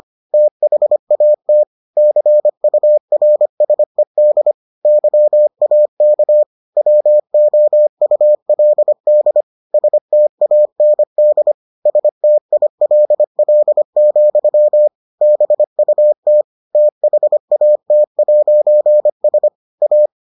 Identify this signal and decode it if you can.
Morse code, heard as a keyed tone, about 25 wpm, 600 Hz. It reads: THAT CURSED YAK WOULD STAND STILL, BUT THAT'S A